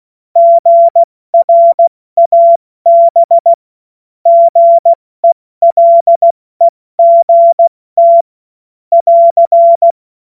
Morse code, heard as a keyed tone, about 16 wpm, 680 Hz. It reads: GRAB GELEGT <AR>
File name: Morsekurs / Witze / Witz 06. September 2022 70 F80